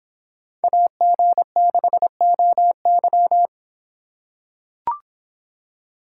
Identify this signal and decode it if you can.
Morse code, heard as a keyed tone, about 25 wpm, 700 Hz. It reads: AG6OY